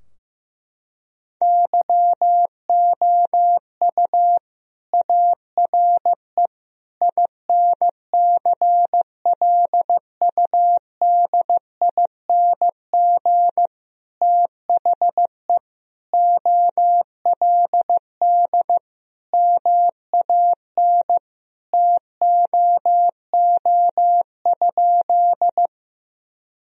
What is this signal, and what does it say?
Morse code, heard as a keyed tone, about 16 wpm, 700 Hz. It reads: YOU ARE INCLUDING THE OLD MAN TOO?
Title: Sentences from Top 1000 Words - Review Part 4 - 15wpm